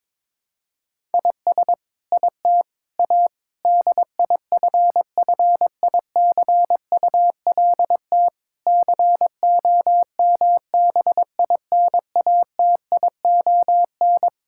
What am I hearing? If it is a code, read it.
Morse code, heard as a keyed tone, about 22 wpm, 700 Hz. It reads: IS IT A DIFFICULT COMBINATION
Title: Sentences from Top 1000 Words - Review Part 6 - 22wpm